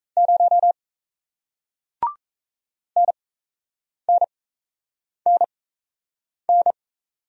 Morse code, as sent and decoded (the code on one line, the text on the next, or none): ----- / -. -. -. -.
0 NNNN